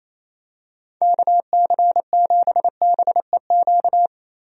Morse code, heard as a keyed tone, about 28 wpm, 700 Hz. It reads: KC7BEQ
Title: USA Call Signs - 6 Characters - Repeated 2x - 28wpm